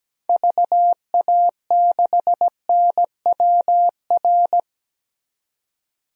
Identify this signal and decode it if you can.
Morse code, heard as a keyed tone, about 17 wpm, 700 Hz. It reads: VA6NWR
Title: DXCC-100 Call Signs - 6 Characters - 17wpm